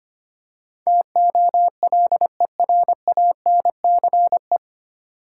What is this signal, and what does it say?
Morse code, heard as a keyed tone, about 25 wpm, 700 Hz. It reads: TOLERANCE
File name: practice / 25wpm / 9 Letter Words - First 3 Letters Given - 25wpm